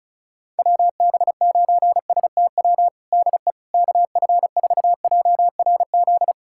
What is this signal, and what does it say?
Morse code, heard as a keyed tone, about 34 wpm, 700 Hz. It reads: WB9STW DE KF4JRZ
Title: USA Callsign DE Callsign 35wpm